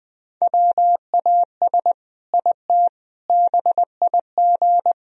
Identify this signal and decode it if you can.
Morse code, heard as a keyed tone, about 20 wpm, 700 Hz. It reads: WAS IT BIG
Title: Sentences from Top 1000 Words - Review Part 1 - 20wpm